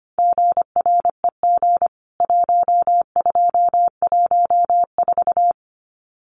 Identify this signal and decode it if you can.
Morse code, heard as a keyed tone, about 25 wpm, 700 Hz. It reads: GREG 1214
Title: CWT Exchange 25wpm